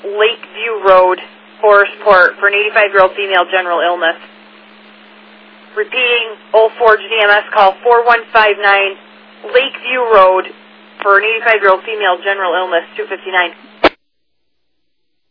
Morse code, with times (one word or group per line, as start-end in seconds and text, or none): none